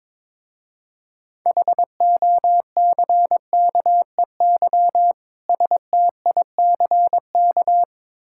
Morse code, heard as a keyed tone, about 22 wpm, 700 Hz. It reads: HOCKEY STICK
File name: Difficult Words with Hints - 22wpm